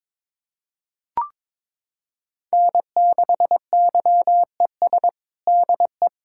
Morse code, heard as a keyed tone, about 22 wpm, 700 Hz. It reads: N6YES DE
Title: K1USN SST QSO Practice 22wpm